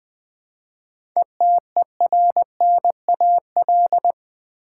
Morse code, heard as a keyed tone, about 20 wpm, 700 Hz. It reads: ETERNAL
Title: 7 Letter Words - First 3 Letters Given - 20wpm